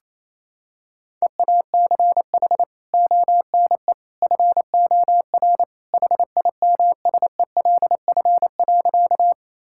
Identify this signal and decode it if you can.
Morse code, heard as a keyed tone, about 28 wpm, 700 Hz. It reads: EACH ONE FOR HIMSELF.
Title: Sentences from Top 1600 Words 28wpm